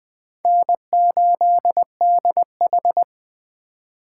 Morse code, heard as a keyed tone, about 20 wpm, 700 Hz. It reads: N8DH
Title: USA Call Signs - 4 Characters - 6x Character Spacing - 20wpm